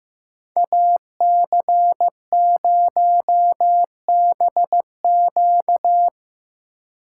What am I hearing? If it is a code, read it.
Morse code, heard as a keyed tone, about 17 wpm, 700 Hz. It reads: AC0BQ
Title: Speed Racing - USA Call Signs 15wpm